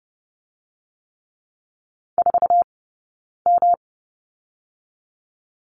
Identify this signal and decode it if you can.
Morse code, heard as a keyed tone, about 30 wpm, 700 Hz. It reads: 4 M
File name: Lesson 18d - Letter m - Two Characters 30wpm